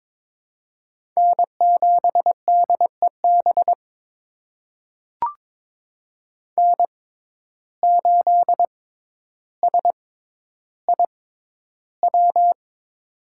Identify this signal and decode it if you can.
Morse code, heard as a keyed tone, about 22 wpm, spaced as for 11 wpm, 700 Hz. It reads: N7DEB N8SIW
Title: USA Call Signs - 5 Characters - 6x Character Spacing - 22wpm